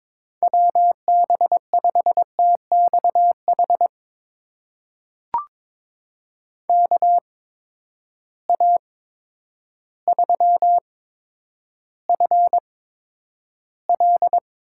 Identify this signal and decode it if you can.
Morse code, heard as a keyed tone, about 22 wpm, spaced as for 10 wpm, 700 Hz. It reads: WB5TXH KA3FL
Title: USA Call Signs - 6 Characters - 8x Character Spacing - 22wpm